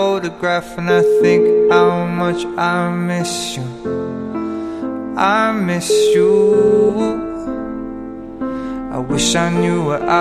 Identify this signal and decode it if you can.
Ringback tone (Europe, 425 Hz), with noise.